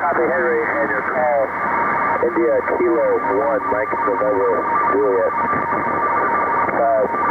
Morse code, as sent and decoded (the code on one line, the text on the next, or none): none